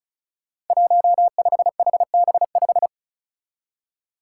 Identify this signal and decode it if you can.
Morse code, heard as a keyed tone, about 34 wpm, 700 Hz. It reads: 15HB5
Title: Five Challenging Letter-Number Combinations 35wpm